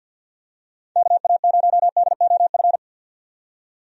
Morse code, heard as a keyed tone, about 52 wpm, 700 Hz. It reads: KA0DOF
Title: USA Call Signs - 6 Characters - 50wpm